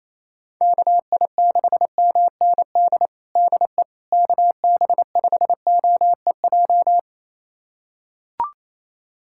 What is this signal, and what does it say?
Morse code, heard as a keyed tone, about 28 wpm, 700 Hz. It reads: KI6MND DE KB5OEJ
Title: USA Callsign DE Callsign 28wpm